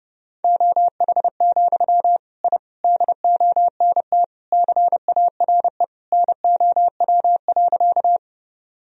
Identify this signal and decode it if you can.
Morse code, heard as a keyed tone, about 32 wpm, 700 Hz. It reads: OH, I DONT CARE NOW.
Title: Sentences from Top 1000 Words - Review Part 1 - 30wpm